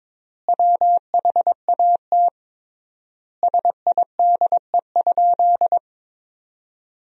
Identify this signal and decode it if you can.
Morse code, heard as a keyed tone, about 22 wpm, 700 Hz. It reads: WHAT SIDE?
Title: 3x Word-Spacing - Sentences from Top 400 Words - 22wpm